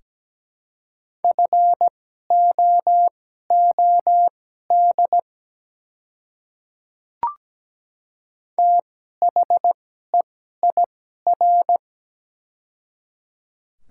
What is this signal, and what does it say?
Morse code, heard as a keyed tone, about 17 wpm, spaced as for 12 wpm, 690 Hz. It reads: FOOD THEIR
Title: 2x Character Spacing - Top 300 Words - 17wpm